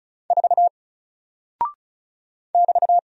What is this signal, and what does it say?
Morse code, heard as a keyed tone, about 36 wpm, 700 Hz. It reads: 4 <BT>
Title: Single Character (Letter:Number:Punctuation:Prosign) 35wpm